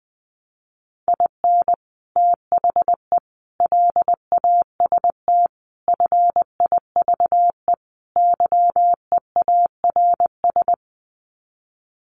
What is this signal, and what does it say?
Morse code, heard as a keyed tone, about 20 wpm, 700 Hz. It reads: IN THE LAST FIVE YEARS
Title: Sets of 5 Words Encore 20wpm